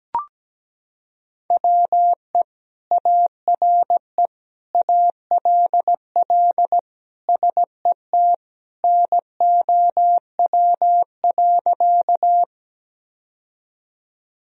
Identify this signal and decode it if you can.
Morse code, heard as a keyed tone, about 17 wpm, 700 Hz. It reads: WE ARE ALL SET NOW.
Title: Sentences from Top 1000 Words - Review Part 1 - 17wpm